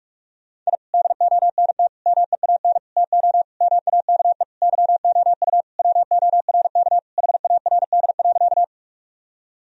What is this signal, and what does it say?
Morse code, heard as a keyed tone, about 41 wpm, 700 Hz. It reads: I DONT MEAN TO MAKE YOU WORK HARD.
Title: Sentences from Top 1000 Words - Review Part 1 - 45wpm